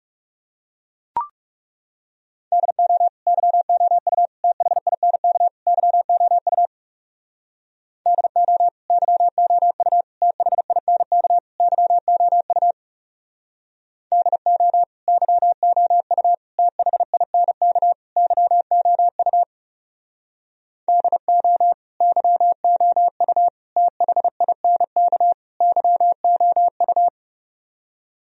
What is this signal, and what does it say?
Morse code, heard as a keyed tone, about 40 wpm, 700 Hz. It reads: DO YOU THINK YOU DO YOU THINK YOU DO YOU THINK YOU DO YOU THINK YOU